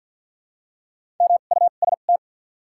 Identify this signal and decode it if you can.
Morse code, heard as a keyed tone, about 54 wpm, spaced as for 37 wpm, 700 Hz. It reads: MUST
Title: 2x Character Spacing - Top 300 Words - 50wpm